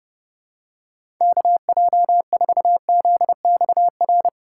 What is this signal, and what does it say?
Morse code, heard as a keyed tone, about 32 wpm, 700 Hz. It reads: KJ4ZXR
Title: USA Call Signs - 6 Characters - 4x Character Spacing - 30wpm